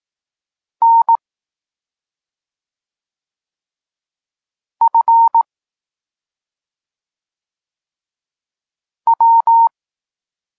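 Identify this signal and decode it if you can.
Morse code, heard as a keyed tone, about 18 wpm, 920 Hz. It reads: N F W
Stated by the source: Polish Morse lesson teaching the letter C, from SP9MOA Niepołomicki Klub Krótkofalowców.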